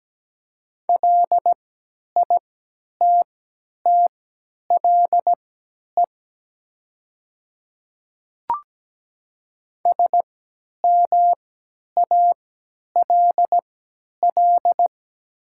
Morse code, heard as a keyed tone, about 17 wpm, spaced as for 10 wpm, 700 Hz. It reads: LITTLE SMALL
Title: 3x Character Spacing - Top 200 Words - 17wpm